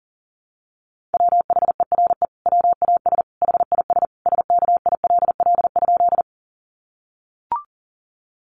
Morse code, heard as a keyed tone, about 40 wpm, 700 Hz. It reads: WHERE WAS HIS SKILL?
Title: Sentences from Top 1200 Words 40wpm